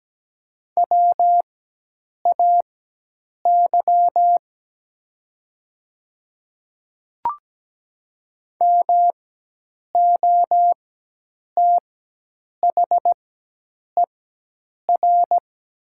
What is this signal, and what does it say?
Morse code, heard as a keyed tone, about 17 wpm, spaced as for 8 wpm, 700 Hz. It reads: WAY MOTHER
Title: 4x Character Spacing - Top 200 Words - 17wpm